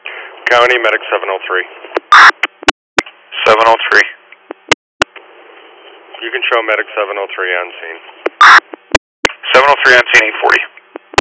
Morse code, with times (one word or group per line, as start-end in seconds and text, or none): none